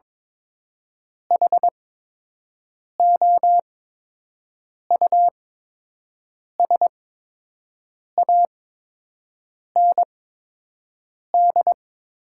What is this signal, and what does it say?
Morse code, heard as a keyed tone, about 21 wpm, spaced as for 6 wpm, 690 Hz. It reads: HOUSAND
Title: 8x Character Spacing - Top 500 Words - 22wpm